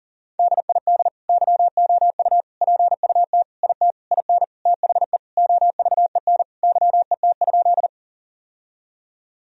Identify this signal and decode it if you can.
Morse code, heard as a keyed tone, about 40 wpm, 700 Hz. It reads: DID YOU PUT IT IN THE OVEN YET?
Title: Sentences from Top 1000 Words - Review Part 4 - 40wpm